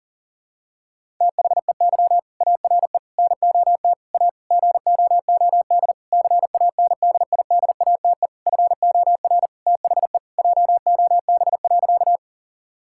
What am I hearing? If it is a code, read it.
Morse code, heard as a keyed tone, about 37 wpm, 700 Hz. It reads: THEY ARE NOT A GOOD CANDIDATE FOR THE JOB.
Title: Sentences from Top 1600 Words 40wpm